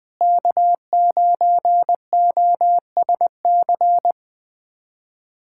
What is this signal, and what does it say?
Morse code, heard as a keyed tone, about 21 wpm, 700 Hz. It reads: K9OSC
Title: USA Callsign DE Callsign - 5 Characters - 20wpm